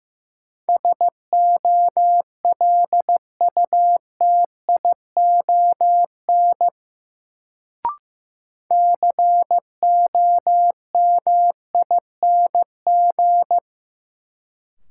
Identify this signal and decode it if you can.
Morse code, heard as a keyed tone, about 15 wpm, 700 Hz. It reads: SOLUTION COMING
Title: Sentences from Top 1300 Words - New Word Drill Alt - 15wpm